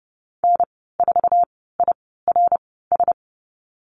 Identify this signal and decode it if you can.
Morse code, heard as a keyed tone, about 30 wpm, spaced as for 17 wpm, 700 Hz. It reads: N4IRS
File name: Lesson 14g - Number 2 - Call Signs 30wpm